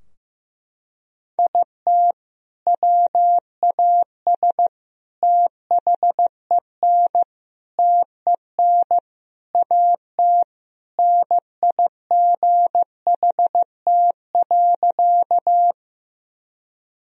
Morse code, heard as a keyed tone, about 16 wpm, 700 Hz. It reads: IT WAS THEN TEN AT NIGHT.
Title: Sentences from Top 1000 Words - Review Part 2 - 15wpm